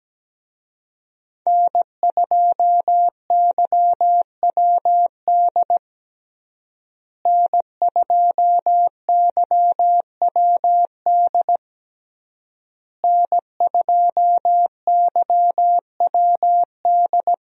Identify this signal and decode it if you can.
Morse code, heard as a keyed tone, about 17 wpm, 700 Hz. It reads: N2YWD N2YWD N2YWD
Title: USA Call Signs - 5 Characters - Repeated 3x - 17wpm